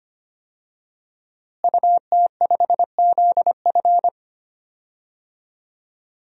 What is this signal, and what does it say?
Morse code, heard as a keyed tone, about 25 wpm, 700 Hz. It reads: UT5ZF
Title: DXCC-100 Call Signs - 5 Characters - Repeated 3x - 25wpm